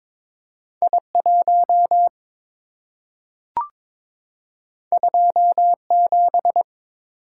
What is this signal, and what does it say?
Morse code, heard as a keyed tone, about 22 wpm, 690 Hz. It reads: I1 27
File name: practice / 22wpm / Two Letter-Number Combinations - 22wpm